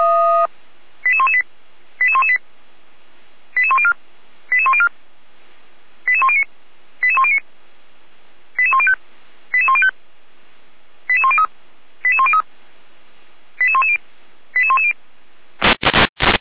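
Five-tone selective call (ZVEI), heard as quick runs of five tones, tones of about 70 ms. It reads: 88118 88118 88104 88104 88109 88109 88105 88105 88103 88103 88110 88110
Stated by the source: ZVEI five-tone selective call received by SDR, from the Feuerwehr Orlamünde fire brigade audio server.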